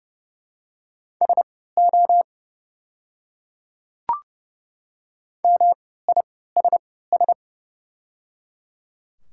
Morse code, heard as a keyed tone, about 29 wpm, spaced as for 17 wpm, 700 Hz. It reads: SO MISS
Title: 3x Character Spacing - Top 500 Words - 30wpm